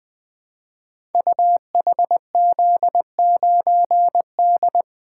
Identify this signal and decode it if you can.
Morse code, heard as a keyed tone, about 20 wpm, 700 Hz. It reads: UHZ9D
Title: Code Groups - 5 Letter:Numbers - 20wpm